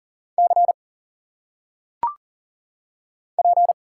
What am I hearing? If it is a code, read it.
Morse code, heard as a keyed tone, about 39 wpm, 700 Hz. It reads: C P
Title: Single Letters - Phonetic Pronunciation 40wpm